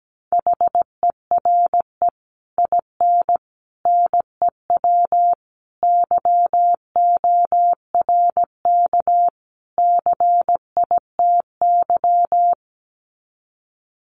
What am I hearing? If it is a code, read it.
Morse code, heard as a keyed tone, about 17 wpm, 700 Hz. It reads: HERE IN NEW YORK CITY
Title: Sets of 5 Words Encore2 17wpm